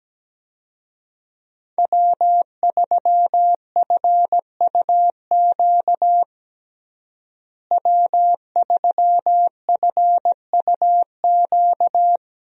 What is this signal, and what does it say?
Morse code, heard as a keyed tone, about 17 wpm, 700 Hz. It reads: W3FUQ W3FUQ
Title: USA Call Signs - 5 Characters - Repeated 2x - 17wpm